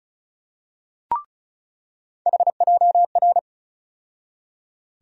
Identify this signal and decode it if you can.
Morse code, heard as a keyed tone, about 34 wpm, 700 Hz. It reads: HJR
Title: USA Call Sign Postfixes - 3 Letters 35wpm